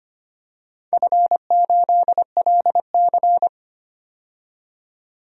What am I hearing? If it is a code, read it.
Morse code, heard as a keyed tone, about 25 wpm, 700 Hz. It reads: F8LC
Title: DXCC-100 Call Signs - 4 Characters - 25wpm